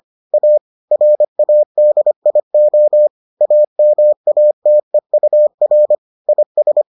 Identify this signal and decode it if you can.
Morse code, heard as a keyed tone, about 25 wpm, 600 Hz. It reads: A RADIO AMATEUR IS